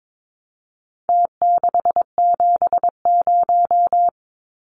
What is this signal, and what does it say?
Morse code, heard as a keyed tone, about 22 wpm, 700 Hz. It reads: T670
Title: Numbers - Contest Style - 4-Digit 22wpm